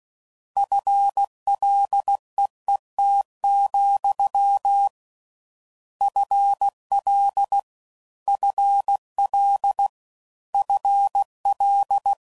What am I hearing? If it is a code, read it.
Morse code, heard as a keyed tone, about 16 wpm, 800 Hz. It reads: FLEET, FL FL FL